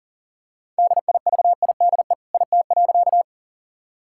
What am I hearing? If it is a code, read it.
Morse code, heard as a keyed tone, about 37 wpm, 700 Hz. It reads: DIVIDE IT.
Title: Sentences from Top 1600 Words 40wpm